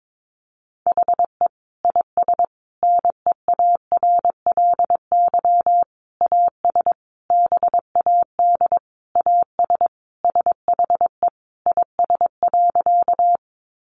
Morse code, heard as a keyed tone, about 22 wpm, 700 Hz. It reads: HE IS NEARLY AS BAD AS SHE IS.
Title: Sentences from Top 1300 Words 22wpm